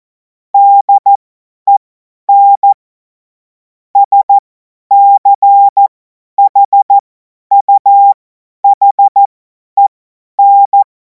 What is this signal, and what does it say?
Morse code, heard as a keyed tone, about 14 wpm, spaced as for 10 wpm, 800 Hz. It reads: DEN SCHUHEN
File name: Morsekurs / Witze / Witz 07. April 2026 50 F70